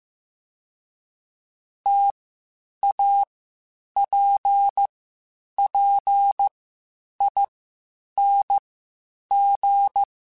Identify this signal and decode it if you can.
Morse code, heard as a keyed tone, about 15 wpm, spaced as for 8 wpm, 800 Hz. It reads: TAPPING
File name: Difficult Combos / Difficult Combos-mp3s / T AOEI @13